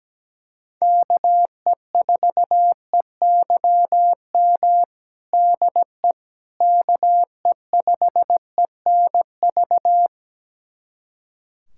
Morse code, heard as a keyed tone, about 17 wpm, 700 Hz. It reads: KE4EYM DE KE5ENV
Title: USA Callsign DE Callsign - 6 Characters - 17wpm